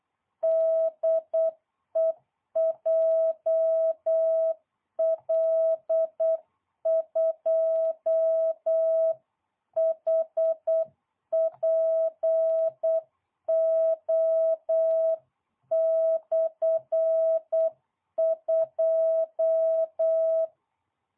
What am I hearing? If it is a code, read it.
Morse code, heard as a keyed tone, about 8 wpm, 650 Hz. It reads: DE JL2HPO/2